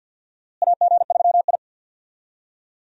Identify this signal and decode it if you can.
Morse code, heard as a keyed tone, about 50 wpm, 700 Hz. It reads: AG3I
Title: USA Call Signs - 4 Characters - Repeated 2x - 50wpm